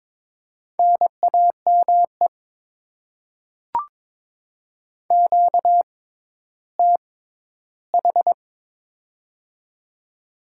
Morse code, heard as a keyed tone, about 22 wpm, spaced as for 12 wpm, 700 Hz. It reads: NAME QTH
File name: Top 20 QSO Elements - 6x Character Spacing - 22wpm